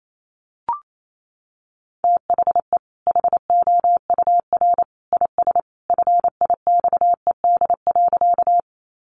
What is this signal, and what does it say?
Morse code, heard as a keyed tone, about 28 wpm, 700 Hz. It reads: THE HOUR IS FIXED.
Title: Sentences from Top 1100 Words 28wpm